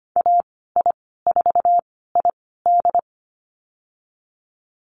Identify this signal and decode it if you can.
Morse code, heard as a keyed tone, about 25 wpm, spaced as for 16 wpm, 700 Hz. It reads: AI4ID
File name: Lesson 15g - Number 5 - Call Signs 25wpm